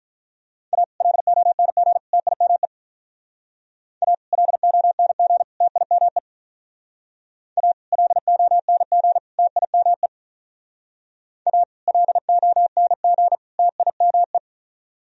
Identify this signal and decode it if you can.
Morse code, heard as a keyed tone, about 44 wpm, 700 Hz. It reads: A LONG TIME A LONG TIME A LONG TIME A LONG TIME